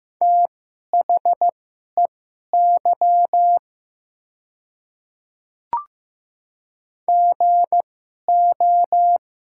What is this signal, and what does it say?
Morse code, heard as a keyed tone, about 15 wpm, spaced as for 11 wpm, 700 Hz. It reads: THEY GO
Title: 2x Character Spacing - Top 200 Words - 15wpm